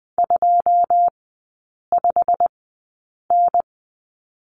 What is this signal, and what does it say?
Morse code, heard as a keyed tone, about 20 wpm, spaced as for 8 wpm, 700 Hz. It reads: 25N
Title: 087 Lesson 15e - Number 5 - Three Characters 20wpm